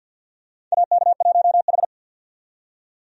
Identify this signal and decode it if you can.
Morse code, heard as a keyed tone, about 50 wpm, 700 Hz. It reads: AK1H
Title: USA Call Signs - 4 Characters - 50wpm